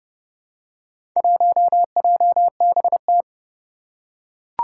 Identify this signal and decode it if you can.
Morse code, heard as a keyed tone, about 28 wpm, 700 Hz. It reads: 1JBT
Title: Four Challenging Letter-Number Combinations 30wpm